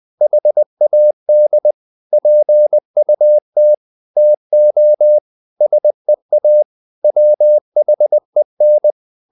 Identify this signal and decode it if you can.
Morse code, heard as a keyed tone, about 20 wpm, 600 Hz. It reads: HAD PUT TO SEA WHEN
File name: the old man and the sea part 2 20wpm